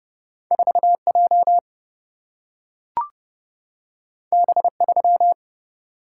Morse code, heard as a keyed tone, about 30 wpm, 700 Hz. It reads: 4J B3
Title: Single-Letter Single-Number 30wpm